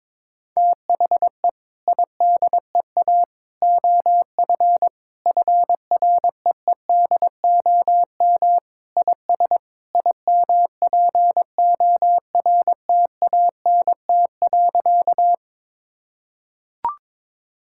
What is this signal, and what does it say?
Morse code, heard as a keyed tone, about 22 wpm, 700 Hz. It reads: THE IDEA OF FREEDOM IS IMPORTANT.
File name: practice / 22wpm / Sentences with 3-Word Phrases - 22wpm